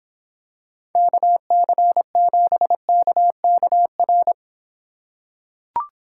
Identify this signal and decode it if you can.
Morse code, heard as a keyed tone, about 25 wpm, 700 Hz. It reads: KC7KKR